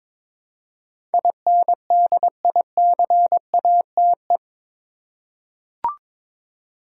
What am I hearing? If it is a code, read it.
Morse code, heard as a keyed tone, about 23 wpm, 700 Hz. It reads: INDICATE
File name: practice / 22wpm / Eight Letter Words - 22wpm